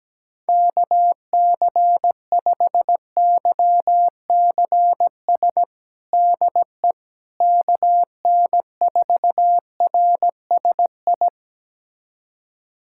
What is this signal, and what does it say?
Morse code, heard as a keyed tone, about 17 wpm, 700 Hz. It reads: KC5YCS DE KN4RSI
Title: USA Callsign DE Callsign - 6 Characters - 17wpm